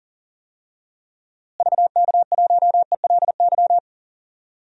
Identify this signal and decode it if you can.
Morse code, heard as a keyed tone, about 40 wpm, 700 Hz. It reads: VK1ELY
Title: DXCC-100 Call Signs - 6 Characters - Repeated 3x - 40wpm